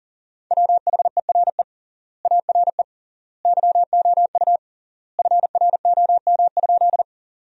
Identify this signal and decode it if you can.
Morse code, heard as a keyed tone, about 40 wpm, 700 Hz. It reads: WHERE ARE YOU FROM?
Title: 3x Word-Spacing - Sentences from Top 200 Words - 40wpm